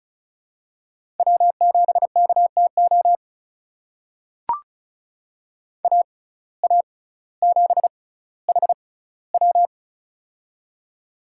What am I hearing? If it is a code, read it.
Morse code, heard as a keyed tone, about 34 wpm, spaced as for 18 wpm, 700 Hz. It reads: W7KTO AA7HW
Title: USA Call Signs - 6x Character Spacing - 35wpm